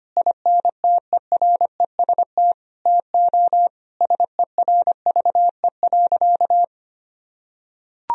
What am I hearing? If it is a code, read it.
Morse code, heard as a keyed tone, about 25 wpm, 700 Hz. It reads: INTEREST TO SERVE.
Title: Sentences from Top 1000 Words - Review Part 4 - 25wpm